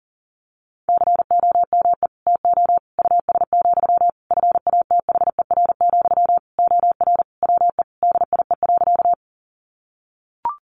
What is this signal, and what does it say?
Morse code, heard as a keyed tone, about 40 wpm, 700 Hz. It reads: COME TO US, FATHER, OR WE DIE.
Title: Sentences from Top 1300 Words 40wpm